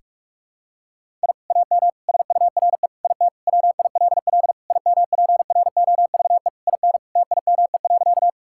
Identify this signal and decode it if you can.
Morse code, heard as a keyed tone, about 46 wpm, 700 Hz. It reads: I AM SURE IT WILL IMPROVE IN TIME.
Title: Sentences from Top 1000 Words - Review Part 5 - 45wpm